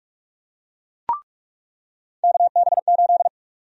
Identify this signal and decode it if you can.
Morse code, heard as a keyed tone, about 46 wpm, 700 Hz. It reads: KB8
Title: USA Call Sign Prefixes - 45wpm